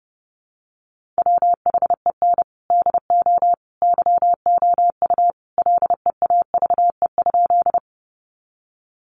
Morse code, heard as a keyed tone, about 30 wpm, 700 Hz. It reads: WHEN DO YOU LEAVE?
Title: Sentences from Top 400 Words 30wpm